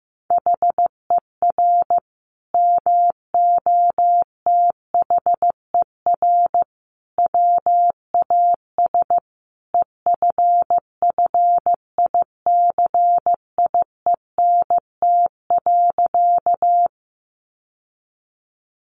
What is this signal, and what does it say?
Morse code, heard as a keyed tone, about 15 wpm, 700 Hz. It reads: HER MOTHER WAS EFFICIENT.
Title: Sentences from Top 1300 Words 15wpm